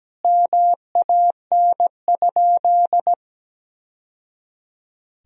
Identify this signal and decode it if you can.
Morse code, heard as a keyed tone, about 17 wpm, 700 Hz. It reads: MAN?
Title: Sentences from Top 1000 Words - Review Part 2 - 17wpm